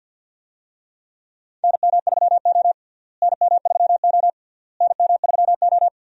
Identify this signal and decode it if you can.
Morse code, heard as a keyed tone, about 52 wpm, 700 Hz. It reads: NM3O NM3O NM3O
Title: USA Call Signs - 4 Characters - Repeated 3x - 50wpm